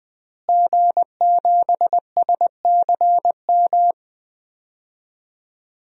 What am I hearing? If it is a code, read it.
Morse code, heard as a keyed tone, about 20 wpm, 700 Hz. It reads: G7SCM
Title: DXCC-100 Call Signs - 5 Characters - Repeated 2x - 20wpm